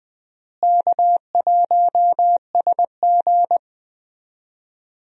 Morse code, heard as a keyed tone, about 20 wpm, 700 Hz. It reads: K1SG
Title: USA Call Signs - 4 Characters - Repeated 2x - 20wpm